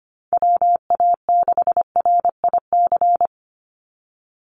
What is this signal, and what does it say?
Morse code, heard as a keyed tone, about 25 wpm, 700 Hz. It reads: WA6RIC
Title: NA SOTA Chaser Callsigns 25wpm